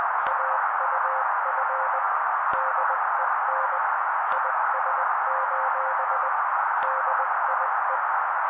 Morse code, heard as a keyed tone, about 20 wpm, 530 Hz. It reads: AUF DEN IS: DIE
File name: daily / 18wpm noise